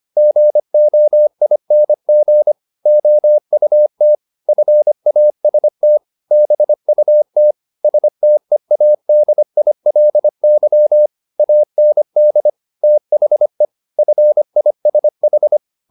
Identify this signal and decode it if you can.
Morse code, heard as a keyed tone, about 25 wpm, 600 Hz. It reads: GOING OUT FAST BUT STEADILY AND THE FISH
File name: the old man and the sea part 1 25wpm